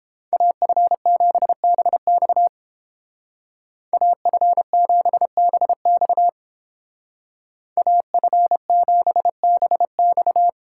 Morse code, heard as a keyed tone, about 30 wpm, 700 Hz. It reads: AF7BX AF7BX AF7BX